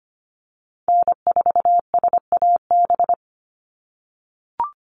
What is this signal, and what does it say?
Morse code, heard as a keyed tone, about 25 wpm, 700 Hz. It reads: N4SAB
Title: USA Call Signs 25wpm